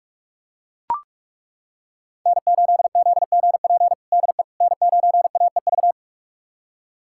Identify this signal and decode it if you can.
Morse code, heard as a keyed tone, about 41 wpm, 700 Hz. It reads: N8ZGP DE N9AEV
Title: USA Callsign DE Callsign - 5 Characters - 45wpm